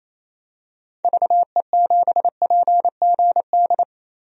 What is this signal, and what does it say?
Morse code, heard as a keyed tone, about 28 wpm, 700 Hz. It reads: VE7PGD
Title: DXCC-100 Call Signs - 6 Characters - 28wpm